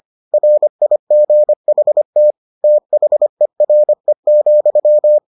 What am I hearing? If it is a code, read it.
Morse code, heard as a keyed tone, about 25 wpm, 600 Hz. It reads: RIGHT THERE,